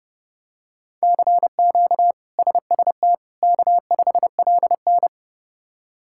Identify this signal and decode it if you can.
Morse code, heard as a keyed tone, about 30 wpm, 700 Hz. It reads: CQ SST K5LN
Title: K1USN SST QSO Practice - 30wpm